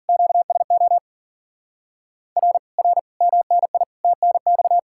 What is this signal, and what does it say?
Morse code, heard as a keyed tone, about 49 wpm, 700 Hz. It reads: QSO R R MNI TNX
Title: Speed-Racing Common QSO Phrases - 40wpm